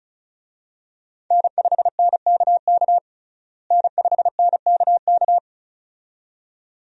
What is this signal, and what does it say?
Morse code, heard as a keyed tone, about 34 wpm, 700 Hz. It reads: N5NKK N5NKK
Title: USA Call Signs - 5 Characters - Repeated 2x - 35wpm